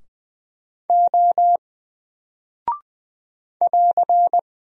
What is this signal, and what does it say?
Morse code, heard as a keyed tone, about 20 wpm, 700 Hz. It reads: O <AR>
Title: Single Character (Letter:Number:Punctuation:Prosign) 20wpm